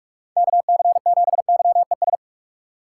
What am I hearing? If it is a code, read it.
Morse code, heard as a keyed tone, about 47 wpm, 700 Hz. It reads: KC7YES
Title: USA Call Signs - 2x Character Spacing - 45wpm